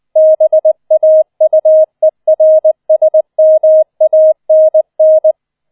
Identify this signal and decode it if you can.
Morse code, heard as a keyed tone, about 20 wpm, 630 Hz. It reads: BAUERSMANN